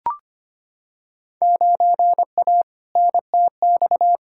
Morse code, heard as a keyed tone, about 25 wpm, 700 Hz. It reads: 9A NTX
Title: Field Day Exchanges Encore - 25wpm